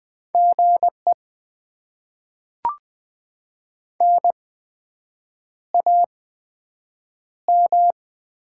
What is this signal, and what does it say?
Morse code, heard as a keyed tone, about 20 wpm, 700 Hz. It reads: G E N A M
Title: Top 20 QSO Elements - 8x Character Spacing - 20wpm